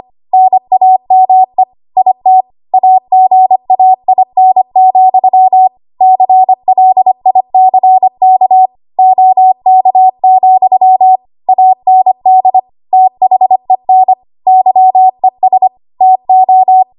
Morse code, heard as a keyed tone, about 25 wpm, 750 Hz. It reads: NAME IT AGAIN, CLICK OK, AND THEN YES TO
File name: Archive / 25 WPM / 25WPM